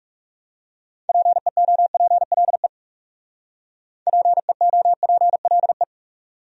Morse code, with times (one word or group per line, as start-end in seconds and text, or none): 1.1-2.7 PEOPLE
4.1-5.8 PEOPLE